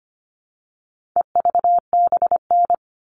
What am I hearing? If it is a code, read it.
Morse code, heard as a keyed tone, about 25 wpm, 700 Hz. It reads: EVBN
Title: All Cut Numbers - 4-Digit 25wpm